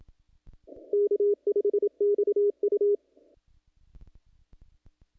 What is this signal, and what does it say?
Morse code, heard as a keyed tone, about 27 wpm, 410 Hz. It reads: K5XU
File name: contests / naqp-cw-k5tr-op-k5pi- -1900